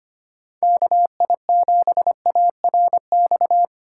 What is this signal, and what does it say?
Morse code, heard as a keyed tone, about 25 wpm, 700 Hz. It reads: KI7ARX
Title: USA Call Signs - 6 Characters - 2x Character Spacing - 25wpm